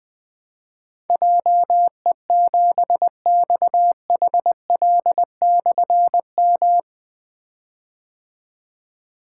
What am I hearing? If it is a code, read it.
Morse code, heard as a keyed tone, about 20 wpm, 700 Hz. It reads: JE7XHL/M
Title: Difficult Call Signs 20wpm